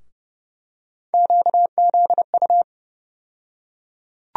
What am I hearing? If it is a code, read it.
Morse code, heard as a keyed tone, about 30 wpm, 700 Hz. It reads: QZU
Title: USA Call Sign Postfixes - 3 Letters 30wpm